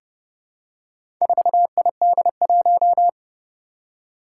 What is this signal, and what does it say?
Morse code, heard as a keyed tone, about 30 wpm, 700 Hz. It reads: 4ID1
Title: Four Challenging Letter-Number Combinations 30wpm